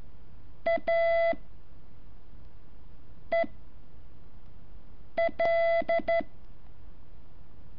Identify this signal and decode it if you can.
Morse code, heard as a keyed tone, about 13 wpm, spaced as for 4 wpm, 670 Hz. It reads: A EL
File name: curso / ejercicio 10